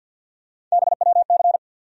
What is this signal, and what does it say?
Morse code, heard as a keyed tone, about 50 wpm, 700 Hz. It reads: BWC
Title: USA Call Sign Postfixes - 3 Letters 50wpm